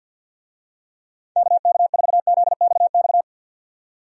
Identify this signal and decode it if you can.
Morse code, heard as a keyed tone, about 46 wpm, 700 Hz. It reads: KK4ZXX
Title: USA Call Signs - 6 Characters - 50wpm